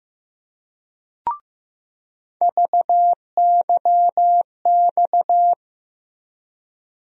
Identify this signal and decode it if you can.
Morse code, heard as a keyed tone, about 15 wpm, 700 Hz. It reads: VYX
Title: USA Call Sign Postfixes - 3 Letters 15wpm